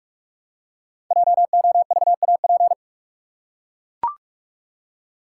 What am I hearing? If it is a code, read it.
Morse code, heard as a keyed tone, about 44 wpm, 700 Hz. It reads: JOVAP
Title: Code Groups - 5 Letters - 45wpm